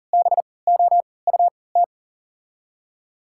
Morse code, heard as a keyed tone, about 40 wpm, spaced as for 23 wpm, 700 Hz. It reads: BOUT
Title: 3x Character Spacing - Top 300 Words - 40wpm